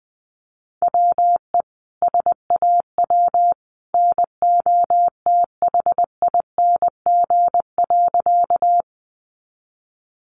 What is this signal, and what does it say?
Morse code, heard as a keyed tone, about 20 wpm, 700 Hz. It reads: WE SAW NOTHING.
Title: A Sound Of Thunder 20wpm